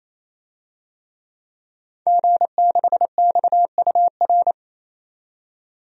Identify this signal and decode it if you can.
Morse code, heard as a keyed tone, about 28 wpm, 700 Hz. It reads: G6XUR
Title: DXCC-100 Call Signs - 5 Characters - 28wpm